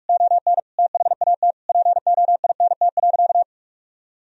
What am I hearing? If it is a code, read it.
Morse code, heard as a keyed tone, about 46 wpm, 700 Hz. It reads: ON THAT POINT.
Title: Sentences from Top 1000 Words - Review Part 1 - 45wpm